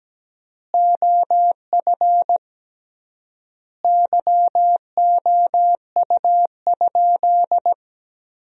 Morse code, heard as a keyed tone, about 17 wpm, 700 Hz. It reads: OF YOU?
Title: 3x Word-Spacing - Sentences from Top 300 Words - 17wpm